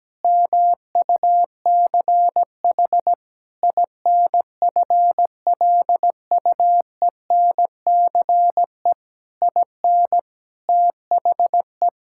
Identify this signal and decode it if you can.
Morse code, heard as a keyed tone, about 17 wpm, 700 Hz. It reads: MUCH INFLUENCE IN THE
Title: Sentences from Top 1600 Words 17wpm